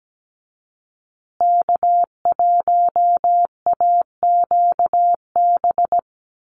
Morse code, heard as a keyed tone, about 17 wpm, 700 Hz. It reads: K1AQB
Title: USA Call Signs Repeated 3x 17wpm